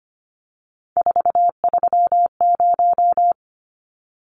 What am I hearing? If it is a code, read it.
Morse code, heard as a keyed tone, about 25 wpm, 700 Hz. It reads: 430
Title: Numbers - 3-Digit 25wpm